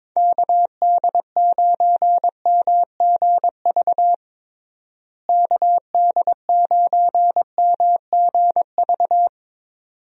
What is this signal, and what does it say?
Morse code, heard as a keyed tone, about 22 wpm, 700 Hz. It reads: KD9MGV KD9MGV
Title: USA Call Signs - 6 Characters - Repeated 2x - 22wpm